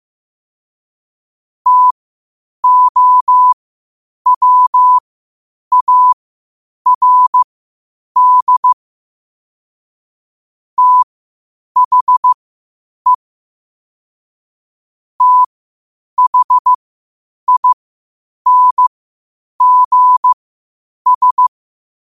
Morse code, heard as a keyed tone, about 15 wpm, spaced as for 8 wpm, 1000 Hz. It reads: TOWARD THE THINGS
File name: qotdfiles / 8wpm